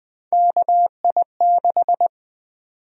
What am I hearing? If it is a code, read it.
Morse code, heard as a keyed tone, about 20 wpm, 700 Hz. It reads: KI6
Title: USA Call Sign Prefixes - 20wpm